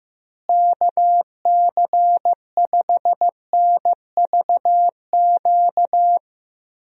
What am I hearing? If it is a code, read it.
Morse code, heard as a keyed tone, about 15 wpm, 700 Hz. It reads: KC5NVQ
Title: USA Call Signs - 6 Characters - Repeated 2x - 15wpm